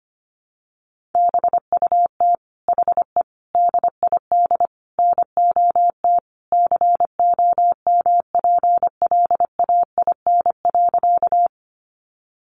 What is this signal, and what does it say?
Morse code, heard as a keyed tone, about 25 wpm, 700 Hz. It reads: BUT HE DID NOT COMPLAIN.
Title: Sentences from Top 1300 Words 25wpm